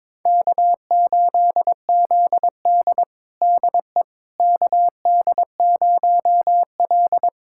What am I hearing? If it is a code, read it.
Morse code, heard as a keyed tone, about 22 wpm, 700 Hz. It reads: K8ZD DE KD0L
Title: USA Callsign DE Callsign - 4 Characters - 22wpm